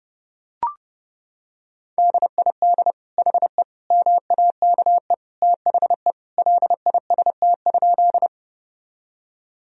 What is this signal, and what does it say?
Morse code, heard as a keyed tone, about 30 wpm, 700 Hz. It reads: DID HE MAKE THE LIST?
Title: Sentences from Top 1000 Words - Review Part 2 - 30wpm